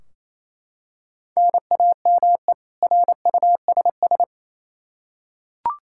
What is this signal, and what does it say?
Morse code, heard as a keyed tone, about 28 wpm, 700 Hz. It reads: NAME RUSS
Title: K1USN SST QSO Practice 28wpm